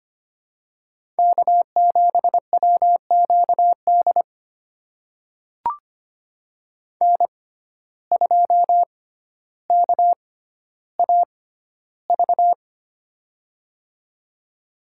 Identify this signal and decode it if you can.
Morse code, heard as a keyed tone, about 25 wpm, spaced as for 13 wpm, 700 Hz. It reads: K7WQD N2KAV
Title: USA Call Signs - 5 Characters - 6x Character Spacing - 25wpm